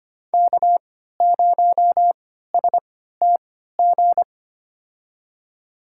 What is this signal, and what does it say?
Morse code, heard as a keyed tone, about 25 wpm, spaced as for 14 wpm, 700 Hz. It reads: K0STG
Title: USA Call Signs - 5 Characters - 3x Character Spacing - 25wpm